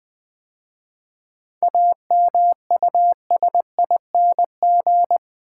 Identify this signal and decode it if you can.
Morse code, heard as a keyed tone, about 20 wpm, 700 Hz. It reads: AMUSING
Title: 7 Letter Words - First 3 Letters Given - 20wpm